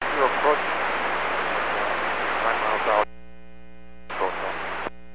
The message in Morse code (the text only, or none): none